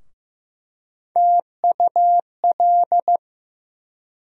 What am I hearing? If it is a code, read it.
Morse code, heard as a keyed tone, about 15 wpm, 700 Hz. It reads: TUL
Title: USA Call Sign Postfixes - 3 Letters 15wpm